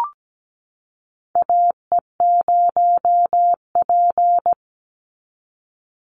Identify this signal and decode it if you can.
Morse code, heard as a keyed tone, about 17 wpm, 700 Hz. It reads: AE0P
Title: Call Signs - 17wpm